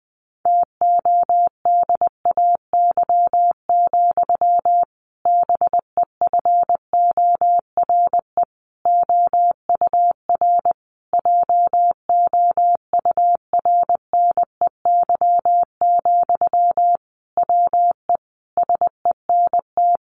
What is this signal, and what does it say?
Morse code, heard as a keyed tone, about 20 wpm, 700 Hz. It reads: TODAY, BEFORE OUR JOURNEY, WE SENT
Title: A Sound Of Thunder 20wpm